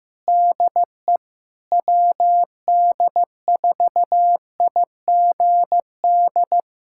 Morse code, heard as a keyed tone, about 15 wpm, 700 Hz. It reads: DE WD4IGD
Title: USA Callsign DE Callsign 15wpm